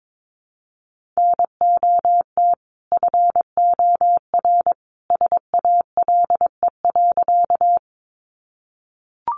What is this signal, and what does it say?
Morse code, heard as a keyed tone, about 22 wpm, 700 Hz. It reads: NOT FOR SALE.
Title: Sentences from Top 1400 Words 22wpm